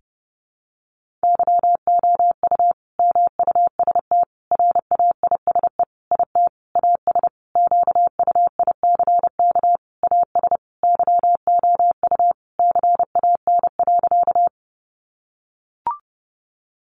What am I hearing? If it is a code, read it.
Morse code, heard as a keyed tone, about 30 wpm, 700 Hz. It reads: YOU MUST RAISE IT AS QUICK AS YOU CAN.
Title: Sentences from Top 1400 Words 30wpm